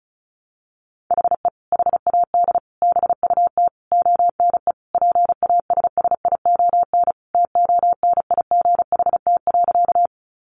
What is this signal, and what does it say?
Morse code, heard as a keyed tone, about 34 wpm, 700 Hz. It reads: HE HAD BUT ONE PASSION TONIGHT.
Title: Sentences from Top 1300 Words 35wpm